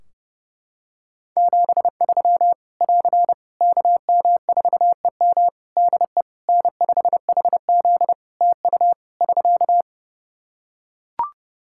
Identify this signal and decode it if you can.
Morse code, heard as a keyed tone, about 30 wpm, 700 Hz. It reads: 73 <AR> KM4EM DE N5HZ TU <SK>
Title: Practice QSO - Exchange 4 - Ending - 30wpm